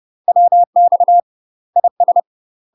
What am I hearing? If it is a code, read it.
Morse code, heard as a keyed tone, about 30 wpm, 700 Hz. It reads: WX IS
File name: UK MASTER QSOs TRAINING / 30WPM / UK MASTER QSOs 0045